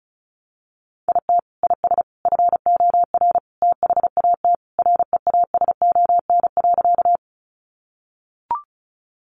Morse code, heard as a keyed tone, about 34 wpm, 700 Hz. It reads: IT IS FOR THAT REASON.
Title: Sentences from Top 700 Words 35wpm